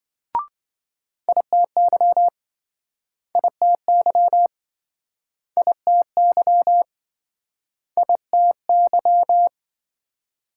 Morse code, heard as a keyed tone, about 28 wpm, 700 Hz. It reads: ITY ITY ITY ITY